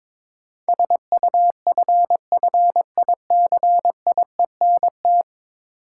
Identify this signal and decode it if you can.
Morse code, heard as a keyed tone, about 22 wpm, 700 Hz. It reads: SUFFICIENT
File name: practice / 22wpm / 10 Letter Words - First 3 Letters Given - 22wpm